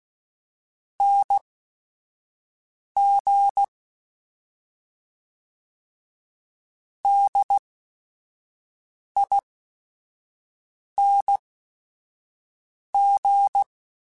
Morse code, heard as a keyed tone, about 16 wpm, spaced as for 5 wpm, 800 Hz. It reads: NG DING